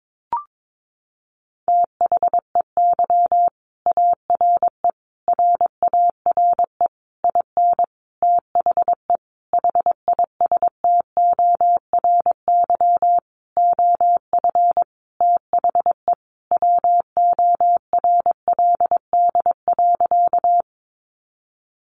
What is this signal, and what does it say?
Morse code, heard as a keyed tone, about 22 wpm, 700 Hz. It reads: THEY ARE RARE IN THE HISTORY OF THE WORLD.
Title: Sentences from Top 1100 Words 22wpm